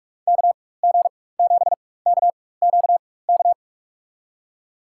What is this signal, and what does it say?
Morse code, heard as a keyed tone, about 42 wpm, spaced as for 20 wpm, 700 Hz. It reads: KG7KQK
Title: USA Call Signs - 6 Characters - 4x Character Spacing - 45wpm